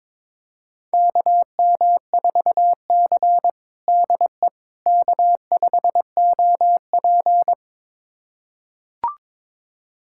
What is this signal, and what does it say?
Morse code, heard as a keyed tone, about 21 wpm, 700 Hz. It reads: KM4C DE K5OP
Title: USA Callsign DE Callsign - 4 Characters - 22wpm